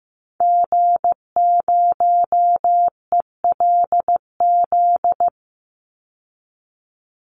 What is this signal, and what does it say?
Morse code, heard as a keyed tone, about 15 wpm, 700 Hz. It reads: G0ELZ
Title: Call Signs Repeated 3x - 15wpm